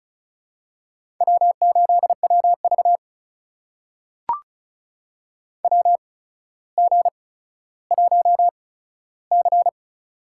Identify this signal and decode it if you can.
Morse code, heard as a keyed tone, about 36 wpm, spaced as for 15 wpm, 700 Hz. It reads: W8WV WG1C
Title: USA Call Signs - 4 Characters - 8x Character Spacing - 35wpm